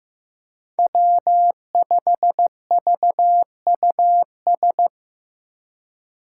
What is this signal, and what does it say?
Morse code, heard as a keyed tone, about 15 wpm, 700 Hz. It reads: W5VUS
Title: Five Challenging Letter-Number Combinations 15wpm